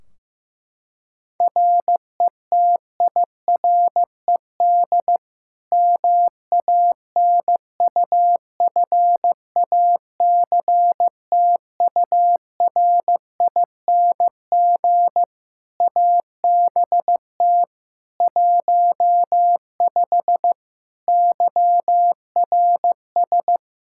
Morse code, heard as a keyed tone, about 15 wpm, 700 Hz. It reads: RETIRED MANUFACTURING ABT 15 YRS
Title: Practice QSO - Exchange 3 - More Info - 15wpm